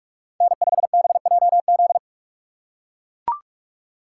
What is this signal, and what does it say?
Morse code, heard as a keyed tone, about 42 wpm, 700 Hz. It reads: N5BJZ